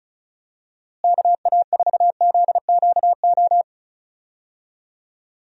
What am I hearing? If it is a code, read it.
Morse code, heard as a keyed tone, about 34 wpm, 700 Hz. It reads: KA4ZQO
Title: USA Call Signs - 6 Characters - Repeated 3x - 35wpm